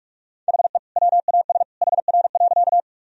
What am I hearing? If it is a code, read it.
Morse code, heard as a keyed tone, about 46 wpm, 700 Hz. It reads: HE WAS HR.
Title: Sentences from Top 200 Words and Abbreviations 45wpm